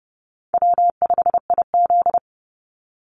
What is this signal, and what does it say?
Morse code, heard as a keyed tone, about 30 wpm, 700 Hz. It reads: W5IZ
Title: Call Signs - 30wpm